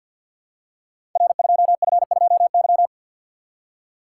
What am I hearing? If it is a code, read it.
Morse code, heard as a keyed tone, about 46 wpm, 700 Hz. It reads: R2F2Y
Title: Code Groups - 5 Letter:Numbers - 50wpm